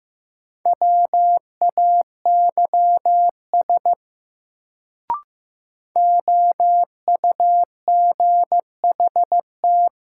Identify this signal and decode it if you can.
Morse code, heard as a keyed tone, about 15 wpm, 700 Hz. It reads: WAYS OUGHT
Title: Sets of 5 Words Encore - New Word Drill - 15wpm